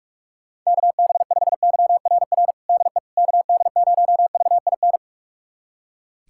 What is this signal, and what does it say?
Morse code, heard as a keyed tone, about 46 wpm, 700 Hz. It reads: KB5YRR DE KD0VIN
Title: USA Callsign DE Callsign 45wpm